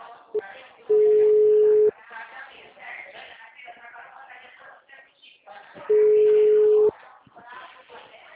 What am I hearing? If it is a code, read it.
Ringback tone (Europe, 425 Hz), with speech.